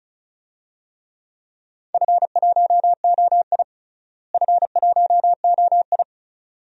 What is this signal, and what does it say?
Morse code, heard as a keyed tone, about 36 wpm, 700 Hz. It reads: F1OI F1OI
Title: DXCC-100 Call Signs - 4 Characters - Repeated 2x - 35wpm